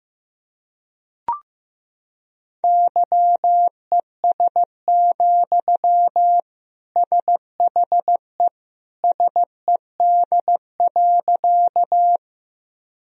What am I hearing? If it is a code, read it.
Morse code, heard as a keyed tone, about 15 wpm, 700 Hz. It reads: YES, SHE SED.
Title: Sentences from Top 500 Words and Abbreviations 15wpm